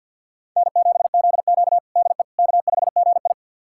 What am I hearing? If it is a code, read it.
Morse code, heard as a keyed tone, about 47 wpm, 700 Hz. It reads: N7ZQ DE K5GI
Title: USA Callsign DE Callsign - 4 Characters - 50wpm